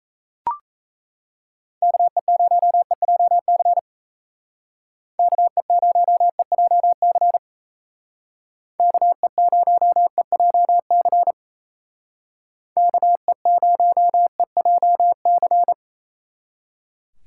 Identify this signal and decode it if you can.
Morse code, heard as a keyed tone, about 33 wpm, 700 Hz. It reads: KE0EJC KE0EJC KE0EJC KE0EJC
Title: Speed Racing - USA Call Signs 28wpm